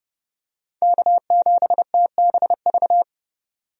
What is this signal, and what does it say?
Morse code, heard as a keyed tone, about 30 wpm, 700 Hz. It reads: K7TBV
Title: USA Call Signs - 5 Characters - 30wpm